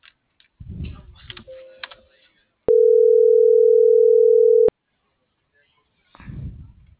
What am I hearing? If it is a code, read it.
Ringback tone (North America), 440 + 480 Hz.
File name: CENTRAL MULTIMODAL / out viridiana.juarez - .82343